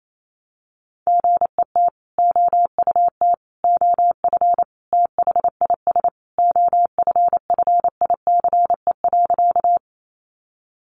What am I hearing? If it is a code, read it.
Morse code, heard as a keyed tone, about 28 wpm, 700 Hz. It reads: GET OUT OF THIS OFFICE.
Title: Sentences from Top 1000 Words 28wpm